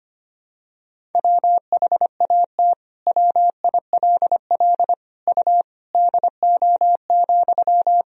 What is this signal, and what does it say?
Morse code, heard as a keyed tone, about 25 wpm, 700 Hz. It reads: WHAT WILL U DO,
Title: Sentences from Top 100 Words and Abbreviations 25wpm